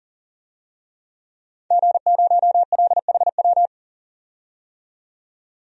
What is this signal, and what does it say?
Morse code, heard as a keyed tone, about 41 wpm, 700 Hz. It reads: G0LHW
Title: DXCC-100 Call Signs - 5 Characters - 40wpm